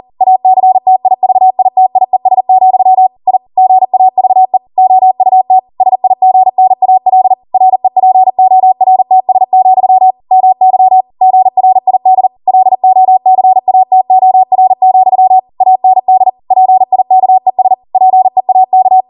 Morse code, heard as a keyed tone, about 40 wpm, 750 Hz. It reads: ACTIVITIES, I GAVE OUT SIGNAL REPORTS, MY GRID LOCATOR, AND PIKES PEAK